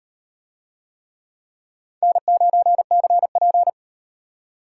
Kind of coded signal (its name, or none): Morse code